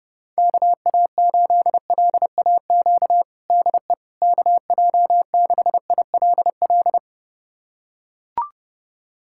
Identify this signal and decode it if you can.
Morse code, heard as a keyed tone, about 32 wpm, 700 Hz. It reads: KA8LAQ DE KJ6ILL
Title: USA Callsign DE Callsign 30wpm